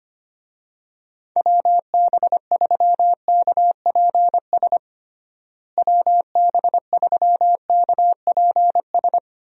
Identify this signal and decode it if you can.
Morse code, heard as a keyed tone, about 25 wpm, 700 Hz. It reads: WB3KPS WB3KPS
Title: USA Call Signs - 6 Characters - Repeated 2x - 25wpm